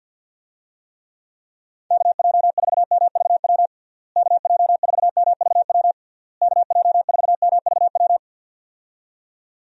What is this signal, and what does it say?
Morse code, heard as a keyed tone, about 52 wpm, 700 Hz. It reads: KJ4MVW KJ4MVW KJ4MVW
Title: USA Call Signs - 6 Characters - Repeated 3x - 50wpm